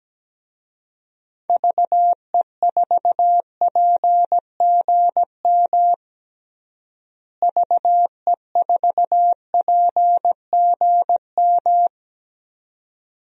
Morse code, heard as a keyed tone, about 17 wpm, 700 Hz. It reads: VE4PGM VE4PGM
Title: DXCC-100 Call Signs - 6 Characters - Repeated 2x - 17wpm